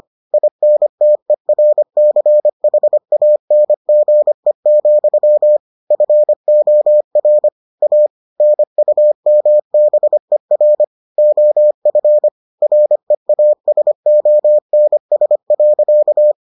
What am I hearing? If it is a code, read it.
Morse code, heard as a keyed tone, about 25 wpm, 600 Hz. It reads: INTERCHANGE, FOR A NUMBER OF REASONS.